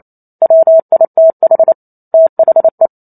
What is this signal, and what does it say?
Morse code, heard as a keyed tone, about 28 wpm, 650 Hz. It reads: WITH THE